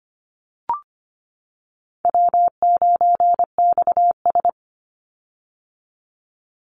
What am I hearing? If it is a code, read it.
Morse code, heard as a keyed tone, about 25 wpm, 700 Hz. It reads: W9XS
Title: Call Signs - 25wpm